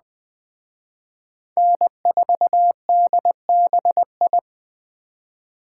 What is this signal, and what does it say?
Morse code, heard as a keyed tone, about 20 wpm, 700 Hz. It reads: N4DBI